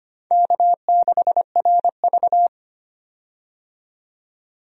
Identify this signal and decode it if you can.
Morse code, heard as a keyed tone, about 25 wpm, 700 Hz. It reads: K6RV
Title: USA Call Signs - 4 Characters - 25wpm